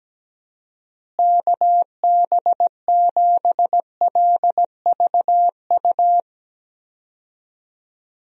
Morse code, heard as a keyed tone, about 17 wpm, 700 Hz. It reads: KB7LVU
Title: USA Call Signs - 6 Characters - Repeated 3x - 17wpm